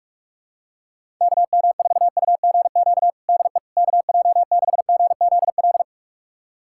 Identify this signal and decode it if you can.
Morse code, heard as a keyed tone, about 41 wpm, 700 Hz. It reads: KM4UGQ DE KJ6GZL
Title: USA Callsign DE Callsign 45wpm